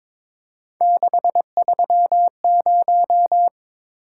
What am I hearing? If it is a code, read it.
Morse code, heard as a keyed tone, about 22 wpm, 700 Hz. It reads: 630
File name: practice / 22wpm / Three Letter-Number Combinations - 22wpm